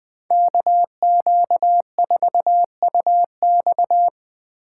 Morse code, heard as a keyed tone, about 20 wpm, 700 Hz. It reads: KQ4UX
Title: Speed Racing - USA 2x2 Call Signs 17wpm